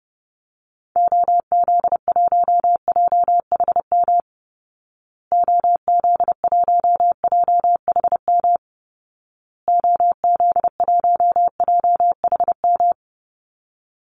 Morse code, heard as a keyed tone, about 30 wpm, 700 Hz. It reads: OZ1JHM OZ1JHM OZ1JHM
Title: Call Signs Repeated 2x - 30wpm